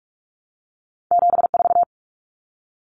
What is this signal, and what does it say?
Morse code, heard as a keyed tone, about 46 wpm, 700 Hz. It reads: 74
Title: Common Cut Numbers - 2-Digit 45wpm